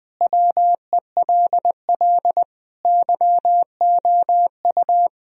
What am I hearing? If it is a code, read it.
Morse code, heard as a keyed tone, about 20 wpm, 700 Hz. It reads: WELL YOU
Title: Sentences from Top 1000 Words - Review Part 3 - 20wpm